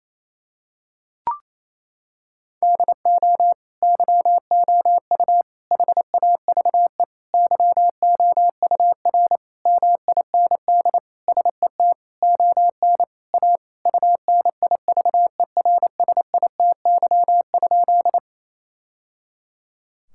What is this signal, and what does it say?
Morse code, heard as a keyed tone, about 28 wpm, 700 Hz. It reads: DO YOU HAVE YOUR MIND SET ON A UNIVERSITY?
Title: Sentences from Top 1000 Words - Review Part 5 - 28wpm